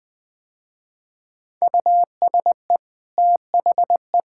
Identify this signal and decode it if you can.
Morse code, heard as a keyed tone, about 20 wpm, 700 Hz. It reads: USE THE
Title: Easy Sentences with Hints - 20wpm